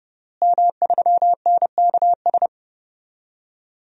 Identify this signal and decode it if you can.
Morse code, heard as a keyed tone, about 30 wpm, 700 Hz. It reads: M3NKS
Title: DXCC-100 Call Signs - 5 Characters - Repeated 3x - 30wpm